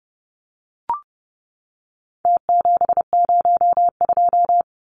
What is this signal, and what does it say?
Morse code, heard as a keyed tone, about 30 wpm, 700 Hz. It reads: T702
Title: Numbers - Contest Style - 4-Digit 30wpm